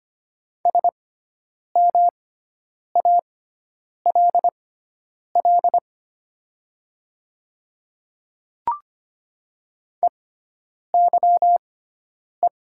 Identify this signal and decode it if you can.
Morse code, heard as a keyed tone, about 25 wpm, spaced as for 9 wpm, 700 Hz. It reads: SMALL EYE
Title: 6x Character Spacing - Top 500 Words - 25wpm